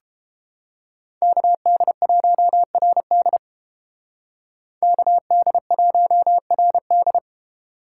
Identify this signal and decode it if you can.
Morse code, heard as a keyed tone, about 29 wpm, 700 Hz. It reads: KD1RD KD1RD